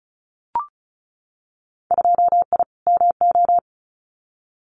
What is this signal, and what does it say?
Morse code, heard as a keyed tone, about 34 wpm, 700 Hz. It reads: 2I MO
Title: Winter Field Day Exchanges 35wpm